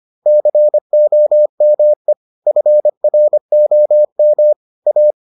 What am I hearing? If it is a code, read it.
Morse code, heard as a keyed tone, about 25 wpm, 600 Hz. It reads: COME FROM A